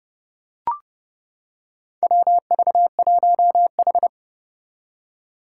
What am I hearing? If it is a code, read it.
Morse code, heard as a keyed tone, about 30 wpm, 700 Hz. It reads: WV1H
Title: Four Challenging Letter-Number Combinations 30wpm